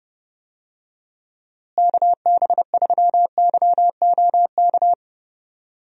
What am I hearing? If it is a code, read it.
Morse code, heard as a keyed tone, about 29 wpm, 700 Hz. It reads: KB3YOK
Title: USA Call Signs - 6 Characters - Repeated 3x - 30wpm